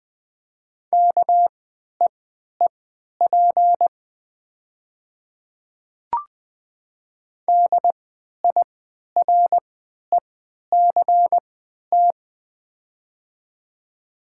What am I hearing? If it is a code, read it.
Morse code, heard as a keyed tone, about 20 wpm, spaced as for 11 wpm, 700 Hz. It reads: KEEP DIRECT
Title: 3x Character Spacing - Top 500 Words - 20wpm